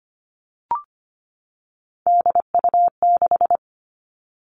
Morse code, heard as a keyed tone, about 25 wpm, 700 Hz. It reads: DU6
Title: All Cut Numbers - 3-Digit 25wpm